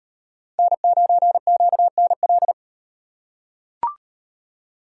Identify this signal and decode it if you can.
Morse code, heard as a keyed tone, about 38 wpm, 700 Hz. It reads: N9QNL